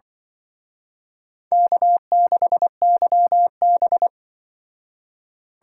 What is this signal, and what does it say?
Morse code, heard as a keyed tone, about 24 wpm, 690 Hz. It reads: K6YB
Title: Speed Racing - USA 1x2 Call Signs 20wpm